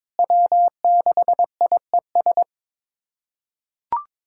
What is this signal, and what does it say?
Morse code, heard as a keyed tone, about 22 wpm, 700 Hz. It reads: W6IES
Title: USA Callsign DE Callsign - 5 Characters - 22wpm